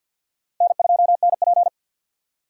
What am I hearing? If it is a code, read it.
Morse code, heard as a keyed tone, about 51 wpm, 700 Hz. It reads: N2NP
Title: USA Call Signs - 4 Characters - 3x Character Spacing - 50wpm